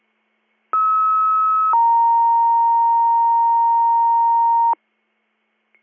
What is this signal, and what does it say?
Two-tone page, heard as a one-second tone, then a three-second tone, about 1290 then 930 Hz. It is heard from 0.7 to 4.7 s.